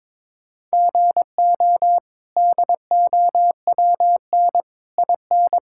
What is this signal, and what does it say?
Morse code, heard as a keyed tone, about 22 wpm, 700 Hz. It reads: GO DOWN IN
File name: practice / 22wpm / Sets of 8 Words - 22wpm